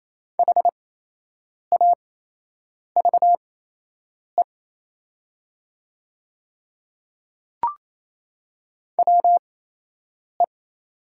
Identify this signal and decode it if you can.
Morse code, heard as a keyed tone, about 28 wpm, spaced as for 8 wpm, 700 Hz. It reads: HAVE WE
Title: 8x Character Spacing - Top 100 Words - 28wpm